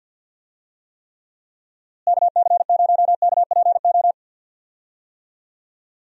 Morse code, heard as a keyed tone, about 46 wpm, 700 Hz. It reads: KC0KPO
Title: USA Call Signs - 6 Characters - Repeated 3x - 50wpm